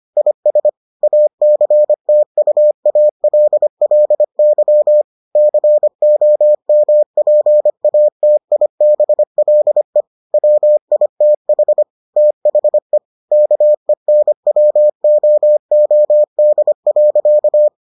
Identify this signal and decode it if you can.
Morse code, heard as a keyed tone, about 25 wpm, 600 Hz. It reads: IS ACTUALLY COMPATIBLE WITH THE KENWOOD.